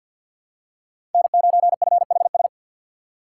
Morse code, heard as a keyed tone, about 47 wpm, 700 Hz. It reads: N9FHS
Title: USA Call Signs - 5 Characters - 3x Character Spacing - 50wpm